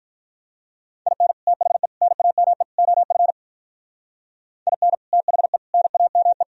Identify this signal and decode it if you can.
Morse code, heard as a keyed tone, about 43 wpm, 700 Hz. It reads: IN THE NAME OF IN THE NAME
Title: Speed-Racing - Sets of 4 Words 35wpm